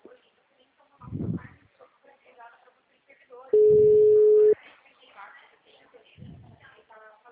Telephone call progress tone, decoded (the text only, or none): ringback tone